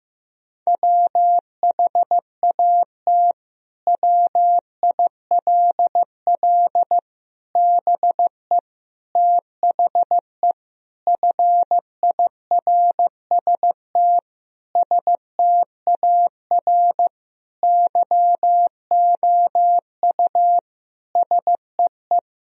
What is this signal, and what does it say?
Morse code, heard as a keyed tone, about 15 wpm, 700 Hz. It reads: WHAT WILL BE THE FIRST STAR YOU SEE
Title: Sentences from Top 1000 Words - Review Part 2 - 15wpm